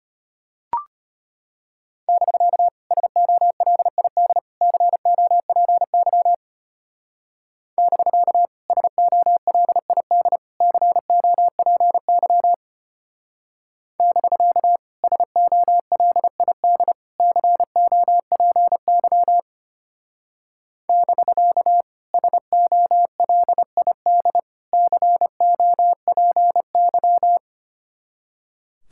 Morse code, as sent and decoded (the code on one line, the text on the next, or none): -...-.- / ... --- .-.. .. -.. / -.-. --- .--. -.-- / -...-.- / ... --- .-.. .. -.. / -.-. --- .--. -.-- / -...-.- / ... --- .-.. .. -.. / -.-. --- .--. -.-- / -...-.- / ... --- .-.. .. -.. / -.-. --- .--. -.--
<BK> SOLID COPY <BK> SOLID COPY <BK> SOLID COPY <BK> SOLID COPY